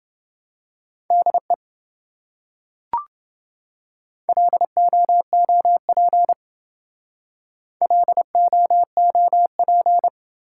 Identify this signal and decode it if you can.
Morse code, heard as a keyed tone, about 26 wpm, 700 Hz. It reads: DE LOOP LOOP